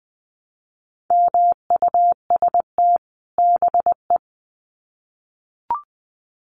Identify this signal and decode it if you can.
Morse code, heard as a keyed tone, about 20 wpm, 700 Hz. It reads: MUST BE
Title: Sets of 2 Words 20wpm